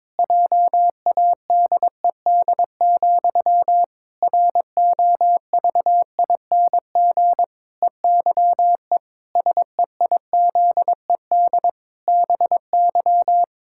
Morse code, heard as a keyed tone, about 21 wpm, 700 Hz. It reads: JADED, ROVING EYE SEIZED BY
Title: Pangram Challenge v2 - 22wpm